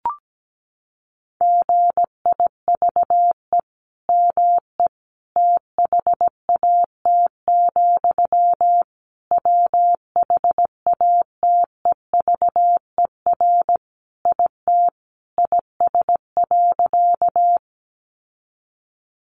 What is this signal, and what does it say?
Morse code, heard as a keyed tone, about 17 wpm, 700 Hz. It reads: GIVE ME THAT, WHATEVER IT IS.
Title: Sentences from Top 1100 Words 17wpm